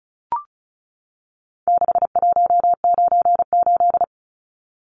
Morse code, heard as a keyed tone, about 34 wpm, 700 Hz. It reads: 6198
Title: Numbers - 4-Digit 35wpm